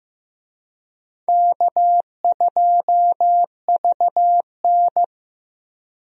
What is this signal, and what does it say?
Morse code, heard as a keyed tone, about 15 wpm, 700 Hz. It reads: K2VN
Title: USA Call Signs - 4 Characters - 15wpm